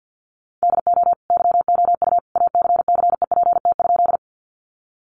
Morse code, heard as a keyed tone, about 50 wpm, 700 Hz. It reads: DO YOU ACCEPT?
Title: Sentences from Top 1300 Words 50wpm